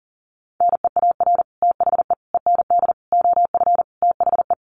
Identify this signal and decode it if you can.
Morse code, heard as a keyed tone, about 40 wpm, 700 Hz. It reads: NEAR THE END OF THE
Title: Sets of 5 Words Encore 40wpm